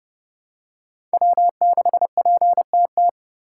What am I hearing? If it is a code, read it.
Morse code, heard as a keyed tone, about 30 wpm, 700 Hz. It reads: W6PTT
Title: USA Call Signs - 5 Characters - 4x Character Spacing - 30wpm